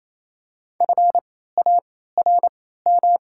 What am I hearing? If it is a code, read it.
Morse code, heard as a keyed tone, about 27 wpm, spaced as for 16 wpm, 700 Hz. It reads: FARM
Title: 3x Character Spacing - Top 500 Words - 28wpm